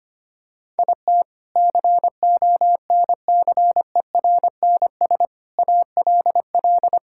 Morse code, heard as a keyed tone, about 25 wpm, 700 Hz. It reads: IT CONCERNS ALL
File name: practice / 25wpm / Sentences with 3-Word Phrases - 25wpm